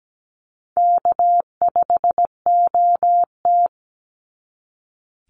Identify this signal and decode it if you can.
Morse code, heard as a keyed tone, about 17 wpm, 700 Hz. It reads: K5OT
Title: NA SOTA Chaser Callsigns 17wpm